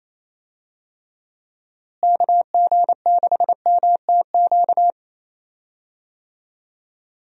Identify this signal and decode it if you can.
Morse code, heard as a keyed tone, about 28 wpm, 700 Hz. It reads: KG6MTQ
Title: USA Call Signs - 6 Characters - Repeated 3x - 28wpm